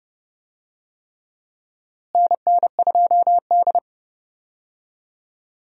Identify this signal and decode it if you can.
Morse code, heard as a keyed tone, about 30 wpm, 700 Hz. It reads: NN2D
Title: USA Call Signs - 4 Characters - Repeated 3x - 30wpm